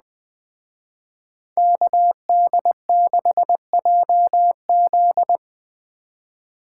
Morse code, heard as a keyed tone, about 20 wpm, 700 Hz. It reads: KD6JZ
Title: Speed Racing - USA 2x2 Call Signs 17wpm